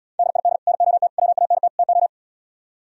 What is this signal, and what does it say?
Morse code, heard as a keyed tone, about 77 wpm, 700 Hz. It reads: DID NOT WANT TO